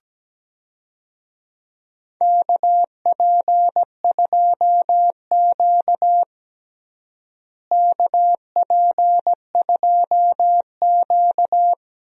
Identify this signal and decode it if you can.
Morse code, heard as a keyed tone, about 17 wpm, 700 Hz. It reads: KP2Q KP2Q
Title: DXCC-100 Call Signs - 4 Characters - Repeated 2x - 17wpm